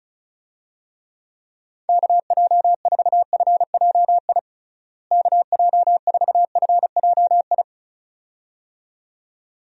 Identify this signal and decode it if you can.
Morse code, heard as a keyed tone, about 36 wpm, 700 Hz. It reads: KJ4FJI KJ4FJI
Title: USA Call Signs - 6 Characters - Repeated 2x - 35wpm